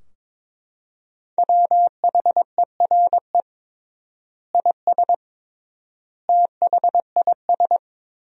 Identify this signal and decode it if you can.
Morse code, heard as a keyed tone, about 22 wpm, 700 Hz. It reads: WHERE IS THIS
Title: 3x Word-Spacing - Sentences from Top 300 Words - 22wpm